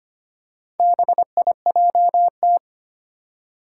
Morse code, heard as a keyed tone, about 25 wpm, 700 Hz. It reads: BIJT
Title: Four Challenging Letter-Number Combinations 25wpm